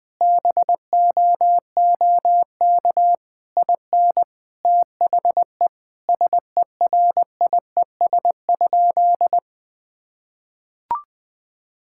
Morse code, heard as a keyed tone, about 21 wpm, 700 Hz. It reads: BOOK IN THE SERIES?
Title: Sentences from Top 1000 Words - Review Part 5 - 20wpm